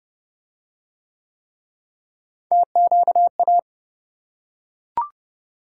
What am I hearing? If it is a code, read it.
Morse code, heard as a keyed tone, about 29 wpm, 700 Hz. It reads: TQA